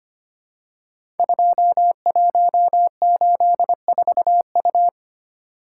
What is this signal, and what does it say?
Morse code, heard as a keyed tone, about 25 wpm, 700 Hz. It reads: 2184U
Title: Code Groups - 5 Letter:Numbers - 25wpm